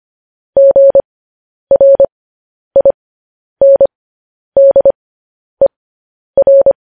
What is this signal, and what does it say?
Morse code, heard as a keyed tone, about 25 wpm, spaced as for 10 wpm, 550 Hz. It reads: GRINDER
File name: podcast / News headlines-1x-25x10